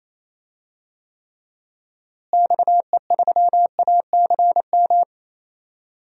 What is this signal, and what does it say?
Morse code, heard as a keyed tone, about 28 wpm, 700 Hz. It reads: XE3ACM
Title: DXCC-100 Call Signs - 6 Characters - Repeated 3x - 28wpm